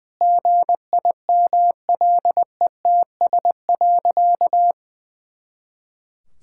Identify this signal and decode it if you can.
Morse code, heard as a keyed tone, about 19 wpm, 700 Hz. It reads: GIMLETS.
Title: Pangram Challenge v2 - 20wpm